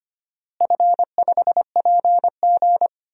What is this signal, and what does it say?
Morse code, heard as a keyed tone, about 25 wpm, 700 Hz. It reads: F5PG
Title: DXCC-100 Call Signs - 4 Characters - Repeated 3x - 25wpm